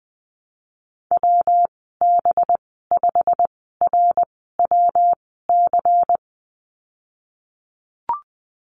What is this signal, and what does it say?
Morse code, heard as a keyed tone, about 21 wpm, spaced as for 15 wpm, 700 Hz. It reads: WB5RWC
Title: 193 Lesson 31g - Letter v - Call Signs 20wpm